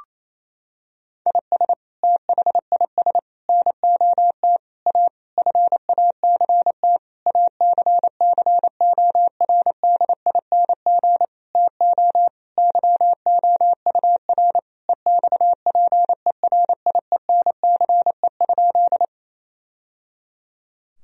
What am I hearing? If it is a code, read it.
Morse code, heard as a keyed tone, about 28 wpm, 700 Hz. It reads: IS THIS NOT A FACT ACCORDING TO YOUR EXPERIENCE?
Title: Sentences from Top 1000 Words - Review Part 4 - 28wpm